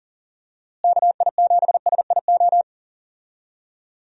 Morse code, heard as a keyed tone, about 40 wpm, 700 Hz. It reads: KI7SIO
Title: USA Call Signs - 6 Characters - Repeated 2x - 40wpm